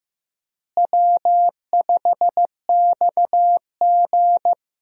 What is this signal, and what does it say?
Morse code, heard as a keyed tone, about 15 wpm, 700 Hz. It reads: W5XG
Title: USA Call Signs - 4 Characters - Repeated 2x - 15wpm